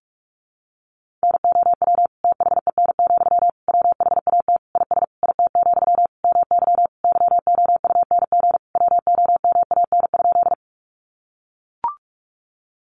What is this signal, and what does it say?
Morse code, heard as a keyed tone, about 44 wpm, 700 Hz. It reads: NOW THEN, WHAT IS IT, MY YOUNG WOMAN?
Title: Sentences from Top 1100 Words 45wpm